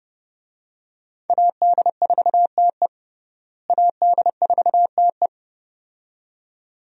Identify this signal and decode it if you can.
Morse code, heard as a keyed tone, about 30 wpm, 700 Hz. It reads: AD4TE AD4TE
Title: USA Call Signs - 5 Characters - Repeated 2x - 30wpm